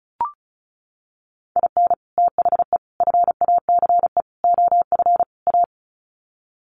Morse code, heard as a keyed tone, about 34 wpm, 700 Hz. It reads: IN THE FACE OF A
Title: Sets of 5 Words Encore 35wpm